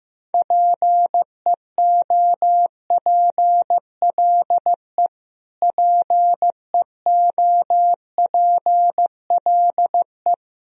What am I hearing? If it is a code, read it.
Morse code, heard as a keyed tone, about 16 wpm, 700 Hz. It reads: PEOPLE PEOPLE
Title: Top 100 Words Repeated 15wpm